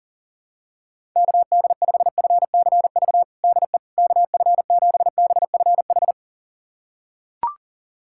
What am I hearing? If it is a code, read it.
Morse code, heard as a keyed tone, about 40 wpm, 700 Hz. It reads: KD5FCV DE KF7BFH
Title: USA Callsign DE Callsign 40wpm